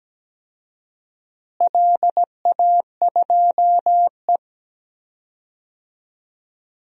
Morse code, heard as a keyed tone, about 17 wpm, 700 Hz. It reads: LA2E
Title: DXCC-100 Call Signs - 4 Characters - Repeated 3x - 17wpm